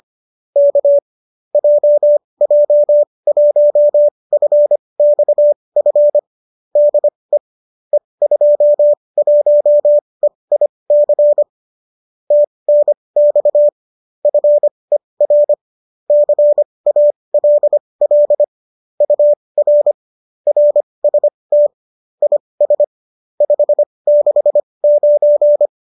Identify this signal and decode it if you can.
Morse code, heard as a keyed tone, about 25 wpm, spaced as for 20 wpm, 590 Hz. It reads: K JJ1FXF DE E21EIC TNX FER CALL UR RST IS 569
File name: qso208 20